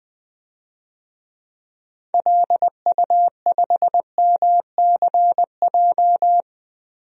Morse code, heard as a keyed tone, about 20 wpm, 700 Hz. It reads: LU5MCJ
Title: DXCC-100 Call Signs - 6 Characters - 20wpm